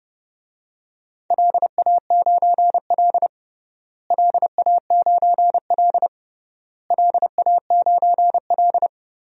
Morse code, heard as a keyed tone, about 30 wpm, 700 Hz. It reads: LA9L LA9L LA9L
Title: DXCC-100 Call Signs - 4 Characters - Repeated 3x - 30wpm